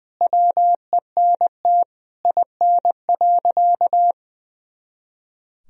Morse code, heard as a keyed tone, about 20 wpm, 700 Hz. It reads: WENT IN.
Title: Sentences from Top 300 Words and Abbreviations 20wpm